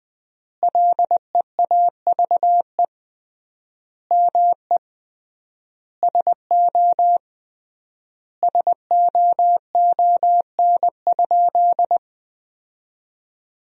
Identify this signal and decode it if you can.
Morse code, heard as a keyed tone, about 21 wpm, 700 Hz. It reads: LEAVE ME SO SOON?
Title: 3x Word-Spacing - Sentences from Top 400 Words - 20wpm